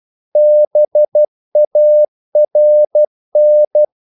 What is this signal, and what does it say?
Morse code, heard as a keyed tone, about 12 wpm, 600 Hz. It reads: BARN
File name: Korte ord DK 12wpm